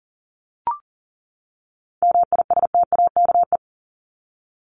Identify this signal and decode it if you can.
Morse code, heard as a keyed tone, about 40 wpm, 700 Hz. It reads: MISTAKE
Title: Top 1501-1750 Words 40wpm